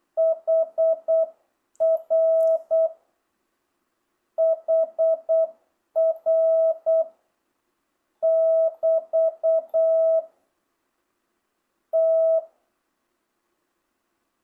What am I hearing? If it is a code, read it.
Morse code, heard as a keyed tone, about 8 wpm, 1300 Hz, with noise. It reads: HR HR <BT> T
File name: jh8cbh / 2025cw